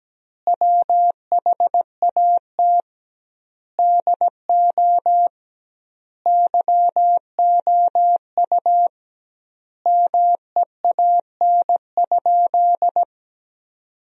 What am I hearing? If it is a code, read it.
Morse code, heard as a keyed tone, about 17 wpm, 700 Hz. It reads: WHAT DO YOU MEAN?
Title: 2x Word-Spacing - Sentences from Top 200 Words - 17wpm